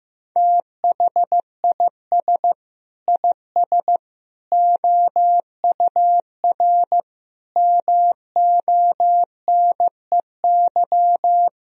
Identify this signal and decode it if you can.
Morse code, heard as a keyed tone, about 15 wpm, 700 Hz. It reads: THIS IS OUR MONEY
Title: Sentences from Top 1000 Words - Review Part 2 - 15wpm